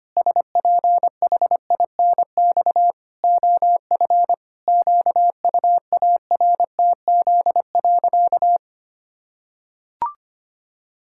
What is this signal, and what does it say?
Morse code, heard as a keyed tone, about 25 wpm, 700 Hz. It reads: SPHINX OF QUARTZ.
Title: Pangram Challenge v2 - 25wpm